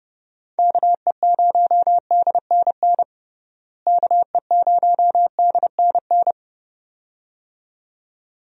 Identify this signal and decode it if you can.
Morse code, heard as a keyed tone, about 30 wpm, 700 Hz. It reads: KE0DNN KE0DNN
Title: USA Call Signs - 6 Characters - Repeated 2x - 30wpm